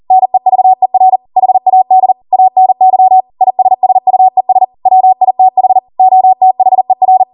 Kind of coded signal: Morse code